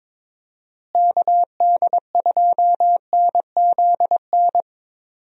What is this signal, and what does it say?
Morse code, heard as a keyed tone, about 21 wpm, 700 Hz. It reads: KD2NZN
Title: USA Call Signs - 6 Characters - Repeated 2x - 22wpm